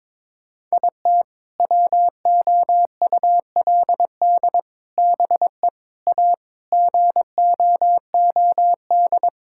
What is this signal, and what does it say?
Morse code, heard as a keyed tone, about 22 wpm, 700 Hz. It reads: IT WOULD BE A GOOD
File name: practice / 22wpm / Sets of 6 Words - 22wpm